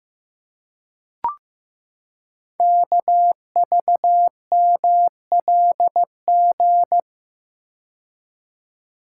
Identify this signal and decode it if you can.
Morse code, heard as a keyed tone, about 15 wpm, 700 Hz. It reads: KVMLG
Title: Code Groups - 5 Letters - 15wpm